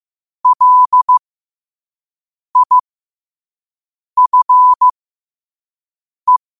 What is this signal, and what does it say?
Morse code, heard as a keyed tone, about 15 wpm, spaced as for 5 wpm, 1000 Hz. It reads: LIFE
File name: qotdfiles / 5wpm